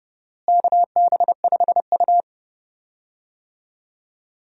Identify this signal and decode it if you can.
Morse code, heard as a keyed tone, about 31 wpm, 700 Hz. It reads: KB5U
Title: USA Call Signs - 4 Characters - Repeated 3x - 30wpm